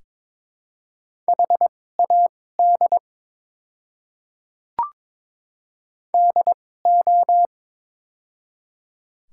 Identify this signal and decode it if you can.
Morse code, heard as a keyed tone, about 22 wpm, spaced as for 16 wpm, 700 Hz. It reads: HAD DO
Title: 2x Character Spacing - Top 100 Words - 22wpm